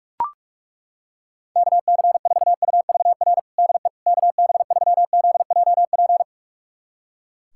Morse code, heard as a keyed tone, about 42 wpm, 700 Hz. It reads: KC4UVR DE KB3ZJP
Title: USA Callsign DE Callsign 45wpm